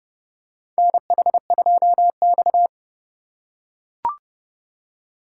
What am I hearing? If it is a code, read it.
Morse code, heard as a keyed tone, about 30 wpm, 700 Hz. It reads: NH2X
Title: USA Call Signs - 4 Characters - 3x Character Spacing - 30wpm